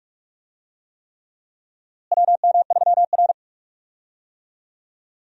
Morse code, heard as a keyed tone, about 41 wpm, 700 Hz. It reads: WM3R
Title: USA Call Signs - 4 Characters - Repeated 3x - 45wpm